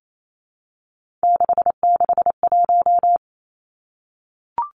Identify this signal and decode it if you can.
Morse code, heard as a keyed tone, about 28 wpm, 700 Hz. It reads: 661
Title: Three Challenging Letter-Number Combinations 28wpm